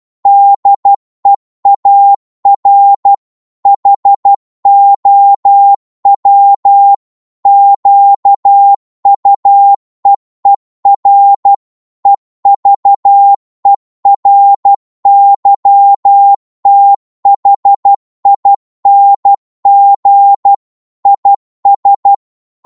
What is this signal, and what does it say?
Morse code, heard as a keyed tone, about 12 wpm, 800 Hz. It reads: DEAR HOW QUEER EVERYTHING IS